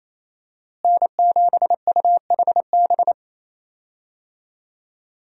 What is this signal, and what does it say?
Morse code, heard as a keyed tone, about 28 wpm, 700 Hz. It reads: N7UHB
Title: USA Call Signs - 5 Characters - Repeated 3x - 28wpm